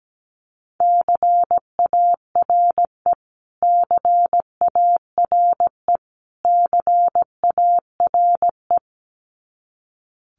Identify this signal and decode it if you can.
Morse code, heard as a keyed tone, about 17 wpm, 700 Hz. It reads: CARE CARE CARE
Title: Top 500 Words Repeated 17wpm